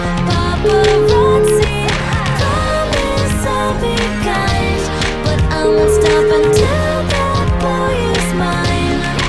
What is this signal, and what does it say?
Ringback tone (Europe, 425 Hz), with noise.